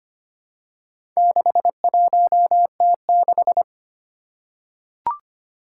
Morse code, heard as a keyed tone, about 25 wpm, 700 Hz. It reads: 61T6
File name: practice / 25wpm / Four Letter-Number Combinations - 25wpm